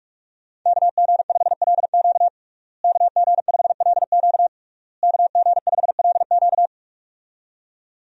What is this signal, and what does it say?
Morse code, heard as a keyed tone, about 43 wpm, 700 Hz. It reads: KG5LQ KG5LQ KG5LQ
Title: USA Call Signs - 5 Characters - Repeated 3x - 45wpm